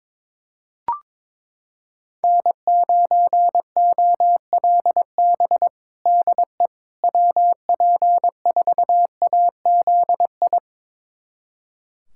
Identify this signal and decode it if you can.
Morse code, heard as a keyed tone, about 22 wpm, 700 Hz. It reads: N9OLB DE WP4AZI
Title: USA Callsign DE Callsign 22wpm